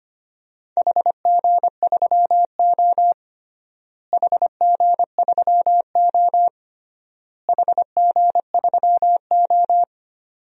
Morse code, heard as a keyed tone, about 25 wpm, 700 Hz. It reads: HG3O HG3O HG3O
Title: DXCC-100 Call Signs - 4 Characters - Repeated 3x - 25wpm